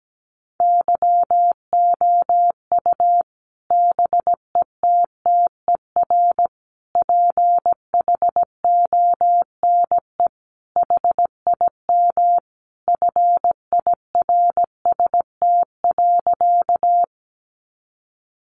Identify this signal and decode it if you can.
Morse code, heard as a keyed tone, about 17 wpm, 700 Hz. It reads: YOU BETTER PHONE HIM FIRST.
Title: Sentences from Top 1100 Words 17wpm